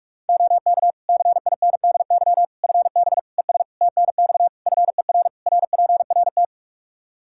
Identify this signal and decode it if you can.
Morse code, heard as a keyed tone, about 46 wpm, 700 Hz. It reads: OK CINDY FB ES TNX FER RPRT
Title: K1USN SST QSO Practice 45wpm